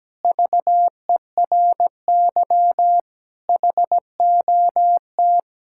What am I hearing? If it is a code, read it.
Morse code, heard as a keyed tone, about 17 wpm, 700 Hz. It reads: VERY HOT
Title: Sentences from Top 1000 Words - Review Part 5 - 17wpm